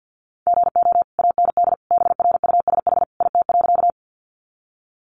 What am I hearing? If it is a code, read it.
Morse code, heard as a keyed tone, about 48 wpm, 700 Hz. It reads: GO AND BRUSH IT.
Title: Sentences from Top 1100 Words 50wpm